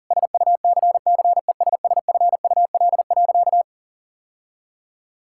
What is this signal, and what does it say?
Morse code, heard as a keyed tone, about 40 wpm, 700 Hz. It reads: SUCCESSFUL.
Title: Sentences from Top 1000 Words - Review Part 6 - 40wpm